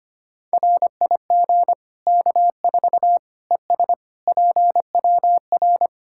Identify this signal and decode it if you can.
Morse code, heard as a keyed tone, about 25 wpm, 700 Hz. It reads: RIG K4 ES PWR
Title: Practice QSO - Exchange 2 - Info - 25wpm